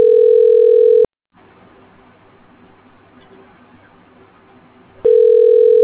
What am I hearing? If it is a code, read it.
Ringback tone (North America), 440 + 480 Hz.
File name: recordings / REfe7c39de656ea80b6adf0991d1ec0560